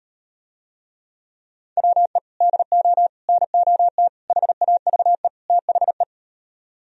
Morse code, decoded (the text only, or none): WE DO NOT HAVE THE